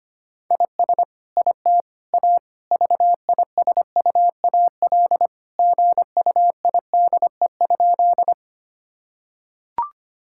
Morse code, heard as a keyed tone, about 25 wpm, 700 Hz. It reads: IS IT A VISUAL GUIDE?
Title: Sentences from Top 1000 Words - Review Part 6 - 25wpm